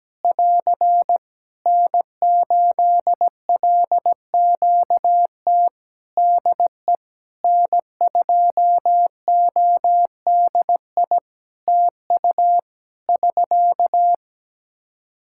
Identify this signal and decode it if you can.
Morse code, heard as a keyed tone, about 17 wpm, 700 Hz. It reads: <AR> N8LQT DE N2ODI TU <SK>
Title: K1USN SST QSO Practice 17wpm